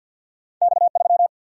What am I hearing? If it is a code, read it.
Morse code, heard as a keyed tone, about 50 wpm, 700 Hz. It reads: X 3
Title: Single-Letter Single-Number 50wpm